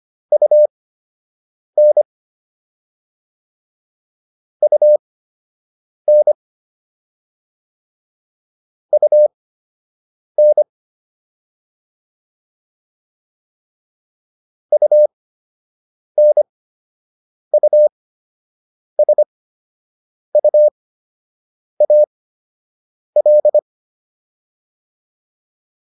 Morse code, heard as a keyed tone, about 25 wpm, spaced as for 7 wpm, 600 Hz. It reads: UN UN UN UNUSUAL